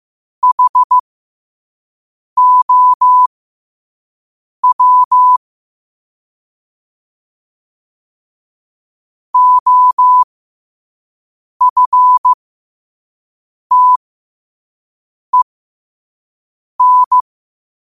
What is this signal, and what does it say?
Morse code, heard as a keyed tone, about 15 wpm, spaced as for 5 wpm, 1000 Hz. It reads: HOW OFTEN
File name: qotdfiles / 5wpm